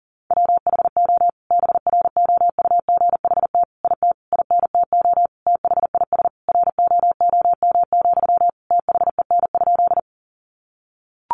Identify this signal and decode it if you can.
Morse code, heard as a keyed tone, about 40 wpm, 700 Hz. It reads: WHO BROUGHT IT INTO THIS ROOM, THEN?
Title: Sentences from Top 600 Words 40wpm